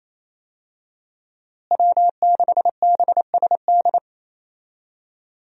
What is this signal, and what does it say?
Morse code, heard as a keyed tone, about 28 wpm, 700 Hz. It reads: W6BSD
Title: USA Call Signs - 5 Characters - Repeated 3x - 28wpm